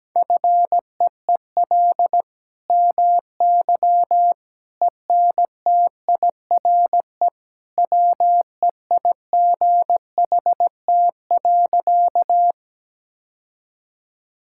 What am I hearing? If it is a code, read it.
Morse code, heard as a keyed tone, about 17 wpm, 700 Hz. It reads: FEEL MY ENTIRE WEIGHT.
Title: Sentences from Top 1000 Words - Review Part 5 - 17wpm